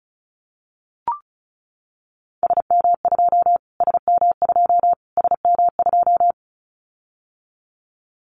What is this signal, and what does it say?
Morse code, heard as a keyed tone, about 34 wpm, 700 Hz. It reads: SM2 SM2 SM2
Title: Callsign Prefixes for Top 25 DXCC - Repeated 3x - 35wpm